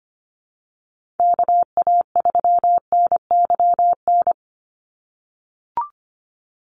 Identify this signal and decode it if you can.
Morse code, heard as a keyed tone, about 25 wpm, 700 Hz. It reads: KA3NYN
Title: USA Call Signs 25wpm